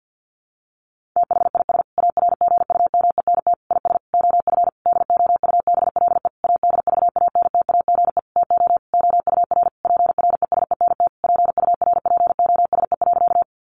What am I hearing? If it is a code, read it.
Morse code, heard as a keyed tone, about 50 wpm, 700 Hz. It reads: THIS ARGUMENT IS OF DOUBLE ADVANTAGE TO OUR PRESENT PURPOSE.
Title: Sentences from Top 1400 Words 50wpm